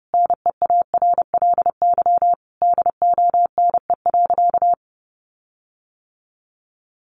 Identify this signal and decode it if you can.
Morse code, heard as a keyed tone, about 28 wpm, 700 Hz. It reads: NEARLY DONE.
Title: Sentences from Top 1300 Words 30wpm